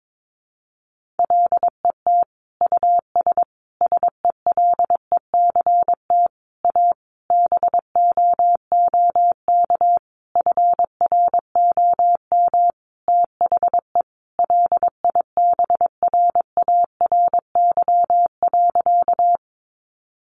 Morse code, heard as a keyed tone, about 22 wpm, 700 Hz. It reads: LET US SELECT A BOOK FROM THE LIBRARY.
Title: Sentences from Top 1300 Words 22wpm